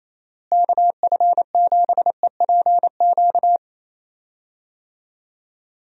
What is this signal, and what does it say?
Morse code, heard as a keyed tone, about 28 wpm, 700 Hz. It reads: KF7EPQ
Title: USA Call Signs - 6 Characters - Repeated 2x - 28wpm